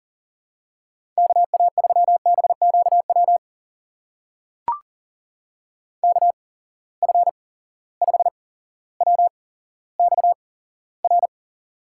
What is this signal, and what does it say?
Morse code, heard as a keyed tone, about 42 wpm, spaced as for 17 wpm, 700 Hz. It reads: KA3BQW KF5WXR
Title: USA Call Signs - 6 Characters - 8x Character Spacing - 40wpm